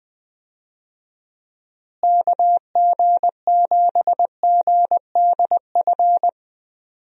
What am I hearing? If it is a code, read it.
Morse code, heard as a keyed tone, about 20 wpm, 700 Hz. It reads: KG7GDF
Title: USA Call Signs - 6 Characters - Repeated 3x - 20wpm